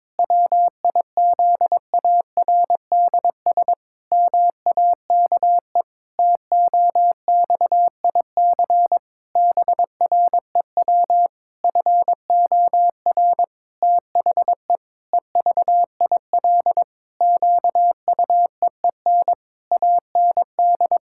Morse code, heard as a keyed tone, about 22 wpm, 700 Hz. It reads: WIZARDS MAKE TOXIC BREW FOR THE EVIL QUEEN AND
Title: Pangram Challenge v2 - 22wpm